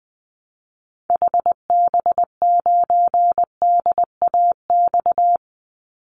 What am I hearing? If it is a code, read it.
Morse code, heard as a keyed tone, about 20 wpm, 700 Hz. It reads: HB9DAX
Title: Call Signs Repeated 2x - 20wpm